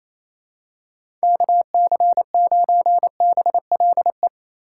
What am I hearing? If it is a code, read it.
Morse code, heard as a keyed tone, about 28 wpm, 700 Hz. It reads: KC9BLE
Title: USA Call Signs - 6 Characters - 3x Character Spacing - 28wpm